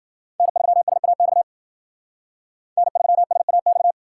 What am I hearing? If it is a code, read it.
Morse code, heard as a keyed tone, about 60 wpm, 700 Hz. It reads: N3SAX N3SAX